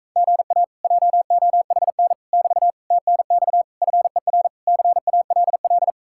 Morse code, heard as a keyed tone, about 44 wpm, 700 Hz. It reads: GA JOHN <BT> TNX FER CALL